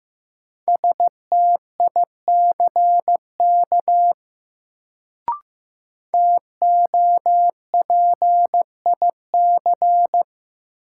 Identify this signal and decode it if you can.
Morse code, heard as a keyed tone, about 15 wpm, 700 Hz. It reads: STICK TOPIC
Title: Sentences from Top 1200 Words - New Word Drill - 15wpm